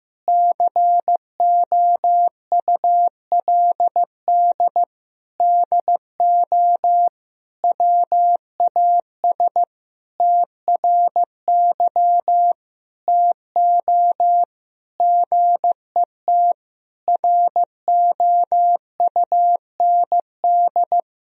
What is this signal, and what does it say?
Morse code, heard as a keyed tone, about 15 wpm, 700 Hz. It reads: COULD DO WAS TRY TO GET ROUND
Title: Sentences from Top 1000 Words - Review Part 1 - 15wpm